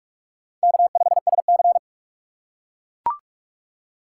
Morse code, heard as a keyed tone, about 42 wpm, 700 Hz. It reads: K5SC
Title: USA Call Signs - 3x Character Spacing - 45wpm